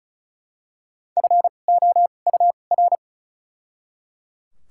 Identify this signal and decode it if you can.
Morse code, heard as a keyed tone, about 34 wpm, 700 Hz. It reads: F O U R
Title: 2x Character Spacing - Top 500 Words - 35wpm